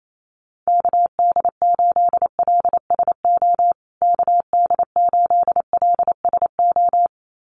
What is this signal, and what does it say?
Morse code, heard as a keyed tone, about 28 wpm, 700 Hz. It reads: KD8LSO KD8LSO
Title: USA Call Signs Repeated 2x 28wpm